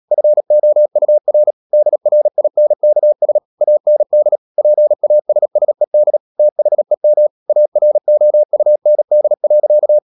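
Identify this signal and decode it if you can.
Morse code, heard as a keyed tone, about 37 wpm, 600 Hz. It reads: FOUR DRINKS AND PASSED THEM AROUND.
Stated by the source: Morse code practice at 35–39 wpm sending English text of Ken Follett's novel Never, from CW med Gnister.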